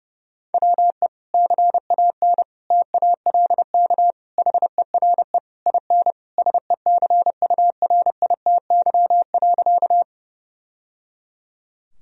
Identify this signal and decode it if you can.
Morse code, heard as a keyed tone, about 30 wpm, 700 Hz. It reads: WE CAN TALK HERE IN SECURITY.
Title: Sentences from Top 1000 Words - Review Part 4 - 30wpm